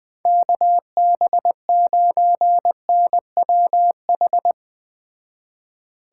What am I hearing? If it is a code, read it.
Morse code, heard as a keyed tone, about 20 wpm, 700 Hz. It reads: KB9NWH